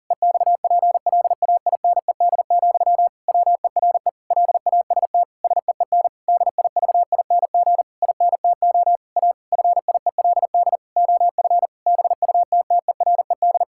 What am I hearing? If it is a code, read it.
Morse code, heard as a keyed tone, about 40 wpm, 700 Hz. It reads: EXPLAINED, WERE LAST SEEN DIVING INTO A FIELD OF BUTTERED
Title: Pangram Challenge v2 - 40wpm